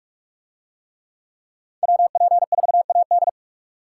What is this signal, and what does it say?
Morse code, heard as a keyed tone, about 43 wpm, 700 Hz. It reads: WP4AD